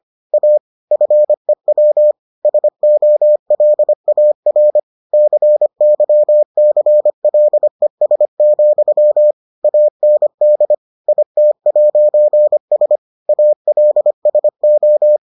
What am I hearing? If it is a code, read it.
Morse code, heard as a keyed tone, about 25 wpm, 600 Hz. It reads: A FEW SOLAR CYCLES, AND IT'S ALSO